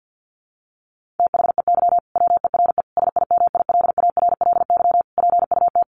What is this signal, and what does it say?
Morse code, heard as a keyed tone, about 50 wpm, 700 Hz. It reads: THEY WERE SIMILARLY PUT
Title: Sentences from Top 1400 Words 50wpm